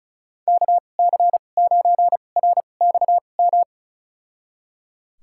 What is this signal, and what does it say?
Morse code, heard as a keyed tone, about 36 wpm, spaced as for 25 wpm, 700 Hz. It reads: KC9RXM
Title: USA Call Signs - 6 Characters - 2x Character Spacing - 35wpm